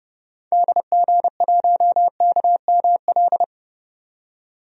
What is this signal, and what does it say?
Morse code, heard as a keyed tone, about 30 wpm, 700 Hz. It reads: DG1KML
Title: DXCC-100 Call Signs - 6 Characters - 30wpm